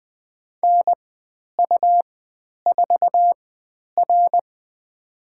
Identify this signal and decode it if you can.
Morse code, heard as a keyed tone, about 20 wpm, spaced as for 10 wpm, 700 Hz. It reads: NU4R
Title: 109 Lesson 18g - Letter m - Call Signs 20wpm